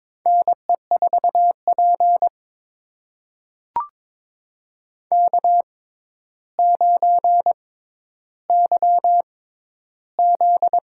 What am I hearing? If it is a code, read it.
Morse code, heard as a keyed tone, about 22 wpm, spaced as for 11 wpm, 700 Hz. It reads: NE4P K9YZ
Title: USA Call Signs - 4 Characters - 6x Character Spacing - 22wpm